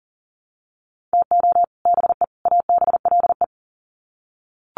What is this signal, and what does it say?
Morse code, heard as a keyed tone, about 40 wpm, 700 Hz. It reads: TO BE ABLE
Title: Sets of 3 Words 40wpm